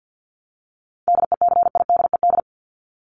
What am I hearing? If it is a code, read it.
Morse code, heard as a keyed tone, about 50 wpm, 700 Hz. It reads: DECIDED
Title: ED Suffix - Top 100 Words with Base Words 50wpm